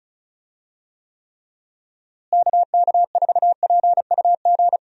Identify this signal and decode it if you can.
Morse code, heard as a keyed tone, about 34 wpm, 700 Hz. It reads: KK4PUG
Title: USA Call Signs - 6 Characters - Repeated 2x - 35wpm